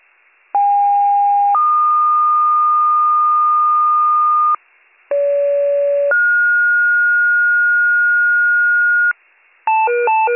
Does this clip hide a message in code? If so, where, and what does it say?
Yes: two-tone page from 0.5 to 9.1 s, heard as a one-second tone, then a three-second tone, about 810 then 1220 Hz, about 570 then 1430 Hz, with noise.